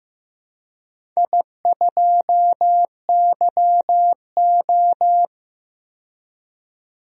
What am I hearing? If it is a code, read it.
Morse code, heard as a keyed tone, about 15 wpm, 700 Hz. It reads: I2YO
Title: DXCC-100 Call Signs - 4 Characters - 15wpm